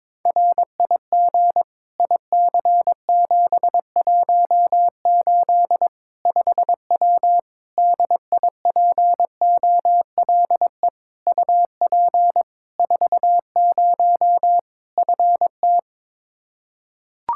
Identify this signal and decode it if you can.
Morse code, heard as a keyed tone, about 21 wpm, 700 Hz. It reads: RIG IC718 5W DIPOLE UP 40 FT
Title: K1USN SST QSO Practice 22wpm